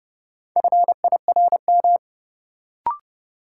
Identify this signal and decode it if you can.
Morse code, heard as a keyed tone, about 30 wpm, 700 Hz. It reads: FIRM
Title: Sentences from Top 1400 Words - New Word Drill - 30wpm